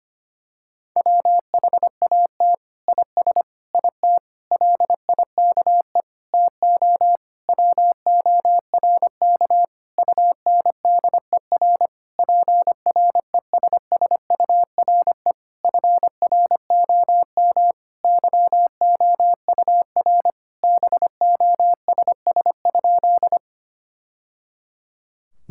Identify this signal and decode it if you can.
Morse code, heard as a keyed tone, about 25 wpm, 700 Hz. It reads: WHAT IS IT LIKE TO WORK UNDER PRESSURE FROM YOUR BOSS?
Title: Sentences from Top 1000 Words - Review Part 6 - 25wpm